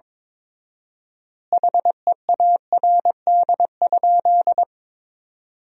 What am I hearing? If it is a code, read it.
Morse code, heard as a keyed tone, about 22 wpm, 700 Hz. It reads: HEARD?
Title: 4x Word-Spacing - Sentences from Top 400 Words - 22wpm